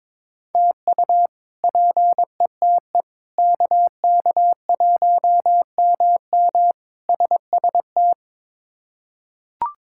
Morse code, heard as a keyed tone, about 21 wpm, 700 Hz. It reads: TU PETE KK1MM SST
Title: K1USN SST QSO Practice - 22wpm